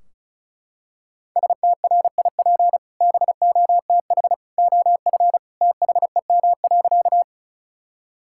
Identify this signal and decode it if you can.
Morse code, heard as a keyed tone, about 34 wpm, 700 Hz. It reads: STRIP BOTH OF THEM.
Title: Sentences from Top 1600 Words 35wpm